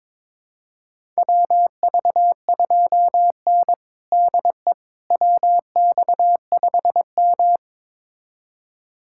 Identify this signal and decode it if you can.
Morse code, heard as a keyed tone, about 22 wpm, 700 Hz. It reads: WV2N DE WX5M
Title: USA Callsign DE Callsign - 4 Characters - 22wpm